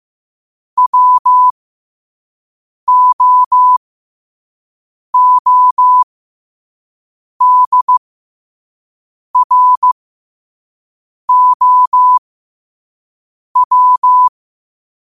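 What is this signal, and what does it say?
Morse code, heard as a keyed tone, about 15 wpm, spaced as for 5 wpm, 1000 Hz. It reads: WOODROW
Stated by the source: Morse quote of the day at 5 wpm.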